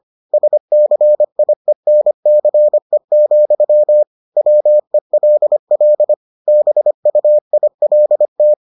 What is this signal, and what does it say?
Morse code, heard as a keyed tone, about 25 wpm, 600 Hz. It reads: SCIENCE, WELL BUILT